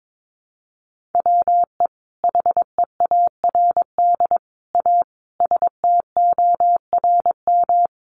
Morse code, heard as a keyed tone, about 22 wpm, 700 Hz. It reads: WE HEARD A STORM
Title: Sentences from Top 1500 Words 22wpm